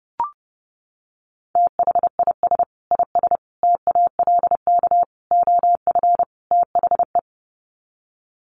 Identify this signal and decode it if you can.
Morse code, heard as a keyed tone, about 30 wpm, 700 Hz. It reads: THIS IS TALK OF THE
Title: Sets of 5 Words 30wpm